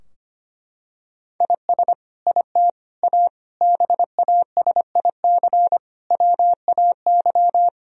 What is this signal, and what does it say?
Morse code, heard as a keyed tone, about 25 wpm, 700 Hz. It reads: IS IT A BASIC WAY
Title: Sentences from Top 1000 Words - Review Part 3 - 25wpm